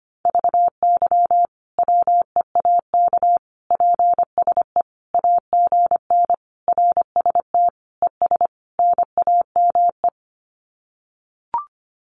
Morse code, heard as a keyed tone, about 25 wpm, 700 Hz. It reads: VY WEAK PSE AGN RST ES NAME
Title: Common QSO Phrases 25wpm